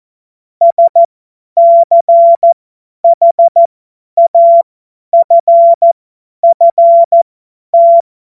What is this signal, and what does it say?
Morse code, heard as a keyed tone, about 14 wpm, spaced as for 10 wpm, 680 Hz. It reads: SCHAFFT
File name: Morsekurs / Witze / Witz 30. Dezember 2025 50 F70